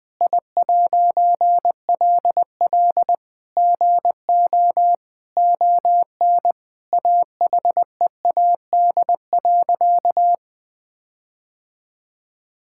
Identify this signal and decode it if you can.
Morse code, heard as a keyed tone, about 21 wpm, 700 Hz. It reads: I'LL GO ON AHEAD.
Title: All-Age Appropriate Jokes - 20wpm